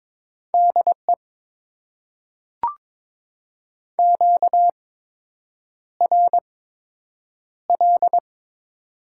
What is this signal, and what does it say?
Morse code, heard as a keyed tone, about 22 wpm, spaced as for 8 wpm, 700 Hz. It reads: DE QRL
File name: Top 90 QSO Elements - 8x Character Spacing - 22wpm